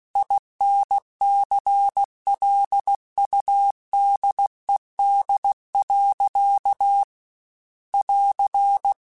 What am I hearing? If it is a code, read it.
Morse code, heard as a keyed tone, about 16 wpm, 800 Hz. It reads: INCLUDED. <AR>